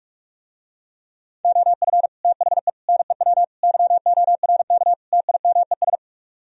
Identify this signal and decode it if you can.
Morse code, heard as a keyed tone, about 44 wpm, 700 Hz. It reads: OF THE NEW YORK TIMES